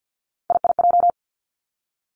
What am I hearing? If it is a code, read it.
Morse code, heard as a keyed tone, about 50 wpm, 700 Hz. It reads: IIJ
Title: Three Challenging Letter-Number Combinations 50wpm